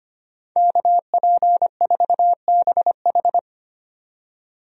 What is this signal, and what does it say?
Morse code, heard as a keyed tone, about 25 wpm, 700 Hz. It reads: KP4BH
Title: DXCC-100 Call Signs - 5 Characters - 25wpm